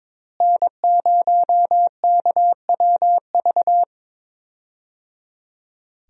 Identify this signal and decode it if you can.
Morse code, heard as a keyed tone, about 22 wpm, 700 Hz. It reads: N0KWV
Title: USA Call Signs - 5 Characters - Repeated 3x - 22wpm